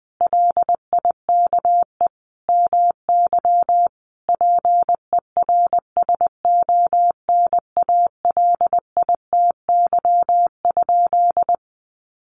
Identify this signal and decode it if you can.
Morse code, heard as a keyed tone, about 20 wpm, 700 Hz. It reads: LIKE MY PERSONALITY?
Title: Sentences from Top 1400 Words 20wpm